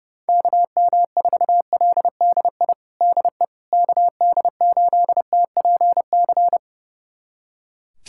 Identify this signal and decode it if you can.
Morse code, heard as a keyed tone, about 30 wpm, 700 Hz. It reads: KM4LDI DE KD8TPC
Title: USA Callsign DE Callsign - 6 Characters - 30wpm